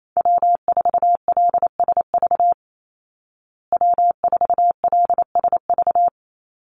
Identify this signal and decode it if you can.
Morse code, heard as a keyed tone, about 28 wpm, 700 Hz. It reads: W4LSV W4LSV
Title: Call Signs Repeated 3x - 28wpm